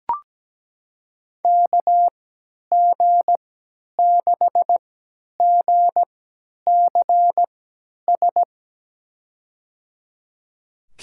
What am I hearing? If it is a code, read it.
Morse code, heard as a keyed tone, about 17 wpm, spaced as for 10 wpm, 700 Hz. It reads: KG6GCS
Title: USA Call Signs - 6 Characters - 3x Character Spacing - 17wpm